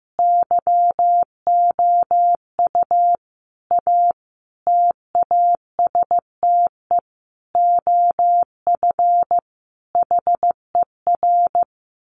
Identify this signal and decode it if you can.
Morse code, heard as a keyed tone, about 15 wpm, 700 Hz. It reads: YOU A TASTE OF HER
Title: Sentences from Top 1100 Words 15wpm